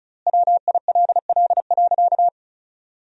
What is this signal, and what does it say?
Morse code, heard as a keyed tone, about 36 wpm, 700 Hz. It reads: WILL.
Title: 4x Word-Spacing - Sentences from Top 100 Words - 35wpm